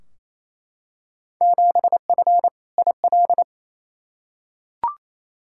Morse code, heard as a keyed tone, about 28 wpm, 700 Hz. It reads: 7F IL
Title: Field Day Exchanges Encore - 28wpm